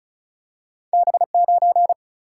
Morse code, heard as a keyed tone, about 36 wpm, 700 Hz. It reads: B9
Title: Single-Letter Single-Number 35wpm